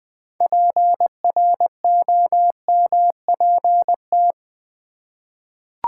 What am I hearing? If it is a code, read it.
Morse code, heard as a keyed tone, about 20 wpm, 700 Hz. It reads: PROMPT
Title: Six Letter Words - 20wpm